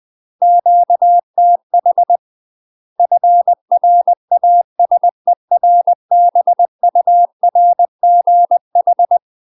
Morse code, heard as a keyed tone, about 20 wpm, 700 Hz. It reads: QTH FRASERBURGH
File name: UK MASTER QSOs TRAINING / 20WPM / UK MASTER QSOs 0011